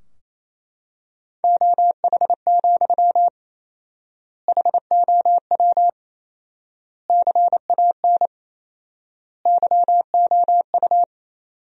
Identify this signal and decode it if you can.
Morse code, heard as a keyed tone, about 28 wpm, 700 Hz. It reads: OH, HOW CAN YOU
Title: 4x Word-Spacing - Sentences from Top 500 Words - 28wpm